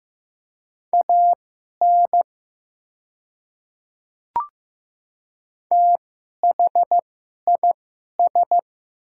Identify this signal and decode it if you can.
Morse code, heard as a keyed tone, about 15 wpm, spaced as for 11 wpm, 700 Hz. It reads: AN THIS
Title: 2x Character Spacing - Top 100 Words - 15wpm